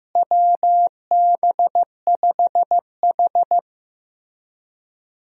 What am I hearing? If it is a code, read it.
Morse code, heard as a keyed tone, about 15 wpm, 700 Hz. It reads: WB5H
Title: USA Call Signs - 4 Characters - Repeated 3x - 15wpm